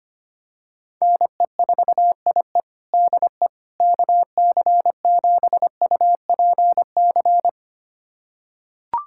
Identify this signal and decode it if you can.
Morse code, heard as a keyed tone, about 25 wpm, 700 Hz. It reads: NE4IE DE KC7UPC
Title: USA Callsign DE Callsign 25wpm